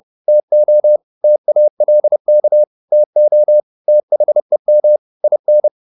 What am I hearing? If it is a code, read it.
Morse code, heard as a keyed tone, about 30 wpm, 600 Hz. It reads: TO TALK TO THEM IN
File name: the old man and the sea / the old man and the sea 1 30wpm